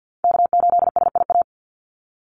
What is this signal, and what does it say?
Morse code, heard as a keyed tone, about 50 wpm, 700 Hz. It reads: K8SIA
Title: Call Signs Repeated 3x - 50wpm